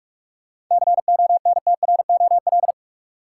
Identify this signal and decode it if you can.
Morse code, heard as a keyed tone, about 46 wpm, 700 Hz. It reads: CONTROL
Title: Sentences from Top 1200 Words - New Word Drill Alt - 45wpm